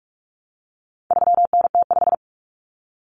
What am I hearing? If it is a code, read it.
Morse code, heard as a keyed tone, about 44 wpm, 700 Hz. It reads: 3NT5
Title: Common Cut Numbers - 4-Digit 45wpm